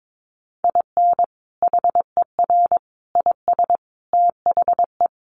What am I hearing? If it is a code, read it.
Morse code, heard as a keyed tone, about 22 wpm, 700 Hz. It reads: IN HER IS THE
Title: Sentences from Top 1400 Words 22wpm